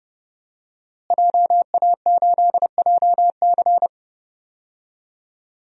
Morse code, heard as a keyed tone, about 30 wpm, 700 Hz. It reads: JA8JC
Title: DXCC-100 Call Signs - 5 Characters - Repeated 2x - 30wpm